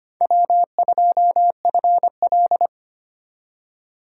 Morse code, heard as a keyed tone, about 25 wpm, 700 Hz. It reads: W2FL
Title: USA Call Signs - 4 Characters - 2x Character Spacing - 25wpm